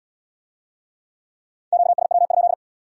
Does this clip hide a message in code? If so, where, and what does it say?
Yes: Morse code from 1.7 to 2.5 s, heard as a keyed tone, about 71 wpm, 700 Hz. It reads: QSW?